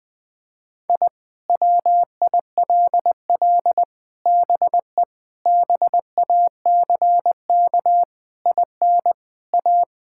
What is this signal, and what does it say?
Morse code, heard as a keyed tone, about 20 wpm, 700 Hz. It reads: I WILL BE BACK IN A
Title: Sentences with 5-Word Phrases - 20wpm